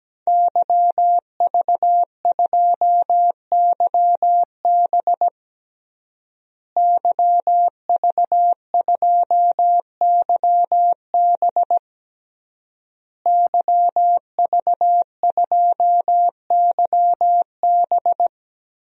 Morse code, heard as a keyed tone, about 17 wpm, 700 Hz. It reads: YV2YB YV2YB YV2YB
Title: DXCC-100 Call Signs - 5 Characters - Repeated 3x - 17wpm